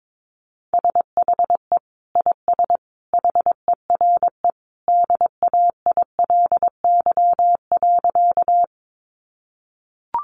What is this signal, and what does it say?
Morse code, heard as a keyed tone, about 22 wpm, 700 Hz. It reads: SHE IS HERE DAILY.
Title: Sentences from Top 1100 Words 22wpm